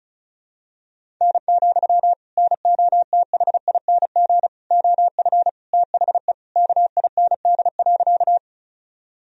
Morse code, heard as a keyed tone, about 34 wpm, 700 Hz. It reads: N, NOTHING OF THE KIND.
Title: Sentences from Top 500 Words and Abbreviations 35wpm